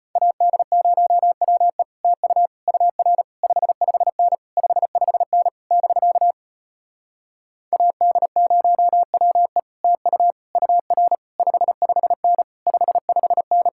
Morse code, heard as a keyed tone, about 33 wpm, 700 Hz. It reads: AD0WE TU UR 55N 55N <BK> AD0WE TU UR 55N 55N